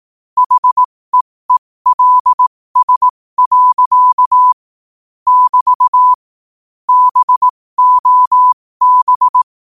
Morse code, heard as a keyed tone, about 18 wpm, spaced as for 15 wpm, 1000 Hz. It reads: HEELS. <BT> BOB
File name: qotdfiles / 15wpm